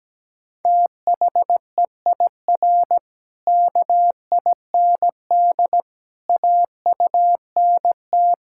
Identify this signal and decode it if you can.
Morse code, heard as a keyed tone, about 17 wpm, 700 Hz. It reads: THEIR KIND AUNT
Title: Pangram Challenge v2 - 17wpm